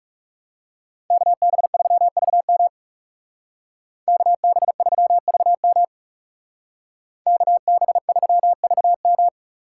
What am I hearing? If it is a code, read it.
Morse code, heard as a keyed tone, about 36 wpm, 700 Hz. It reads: KB3VM KB3VM KB3VM
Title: Speed Racing - USA 2x2 Call Signs 30wpm